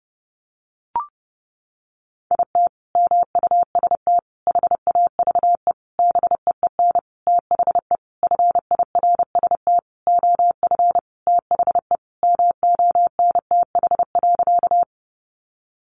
Morse code, heard as a keyed tone, about 30 wpm, 700 Hz. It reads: IT MUST HAVE BEEN THE FIRST OF THE MONTH.
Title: Sentences from Top 800 Words 30wpm